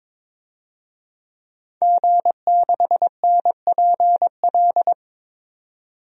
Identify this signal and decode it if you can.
Morse code, heard as a keyed tone, about 22 wpm, 700 Hz. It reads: G6NPL
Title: DXCC-100 Call Signs - 5 Characters - 22wpm